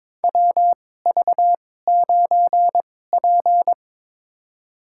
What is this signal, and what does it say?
Morse code, heard as a keyed tone, about 22 wpm, spaced as for 16 wpm, 700 Hz. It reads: WV9P
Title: USA Call Signs - 4 Characters - 2x Character Spacing - 22wpm